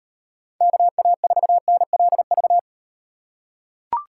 Morse code, heard as a keyed tone, about 38 wpm, 700 Hz. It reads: KA4NLV